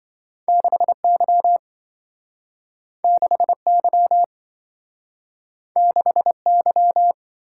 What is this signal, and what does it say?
Morse code, heard as a keyed tone, about 27 wpm, 700 Hz. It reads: 6Y 6Y 6Y